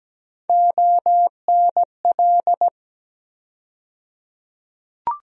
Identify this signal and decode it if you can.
Morse code, heard as a keyed tone, about 17 wpm, 700 Hz. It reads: ONL
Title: USA Call Sign Postfixes - 3 Letters 17wpm